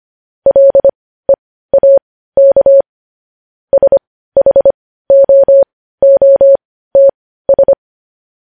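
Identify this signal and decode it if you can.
Morse code, heard as a keyed tone, about 25 wpm, spaced as for 15 wpm, 550 Hz. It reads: LEAK SHOOTS
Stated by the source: Morse code news headlines sent at 15.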